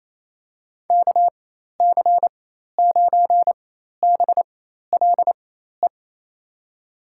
Morse code, heard as a keyed tone, about 28 wpm, spaced as for 13 wpm, 700 Hz. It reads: KC9BLE
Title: USA Call Signs - 4x Character Spacing - 28wpm